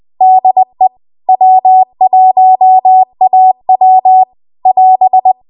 Morse code, heard as a keyed tone, about 20 wpm, 750 Hz. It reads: DE W1AW <AS>
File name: Archive / 20 WPM / 20WPM